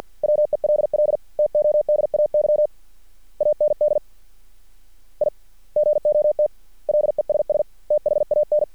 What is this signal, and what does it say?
Morse code, heard as a keyed tone, about 50 wpm, 600 Hz. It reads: WELL TODAY AND I GOT LESS THAN
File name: archived / CRSnet